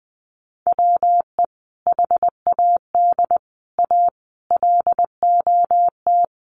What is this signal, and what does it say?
Morse code, heard as a keyed tone, about 20 wpm, 700 Hz. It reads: WE HAD A LOT
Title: Sets of 5 Words Encore 20wpm